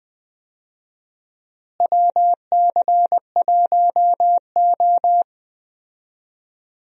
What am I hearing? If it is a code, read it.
Morse code, heard as a keyed tone, about 20 wpm, 700 Hz. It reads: WC1O
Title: USA Call Signs - 4 Characters - Repeated 2x - 20wpm